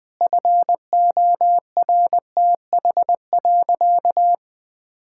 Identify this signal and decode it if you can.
Morse code, heard as a keyed tone, about 20 wpm, 700 Hz. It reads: FORTH.
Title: Sentences with 6-Word Phrases - Spoken Beginning - 20wpm